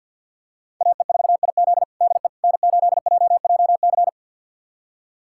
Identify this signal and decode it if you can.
Morse code, heard as a keyed tone, about 47 wpm, 700 Hz. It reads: AE4IZ DE N8JJC
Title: USA Callsign DE Callsign - 5 Characters - 50wpm